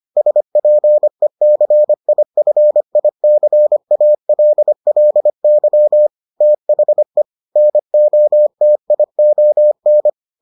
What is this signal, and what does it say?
Morse code, heard as a keyed tone, about 25 wpm, 600 Hz. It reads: SPECIFICALLY THE NOTION